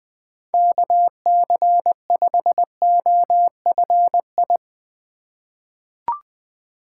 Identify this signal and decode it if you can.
Morse code, heard as a keyed tone, about 21 wpm, 700 Hz. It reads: KC5OFI
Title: USA Call Signs - 6 Characters - 8x Character Spacing - 20wpm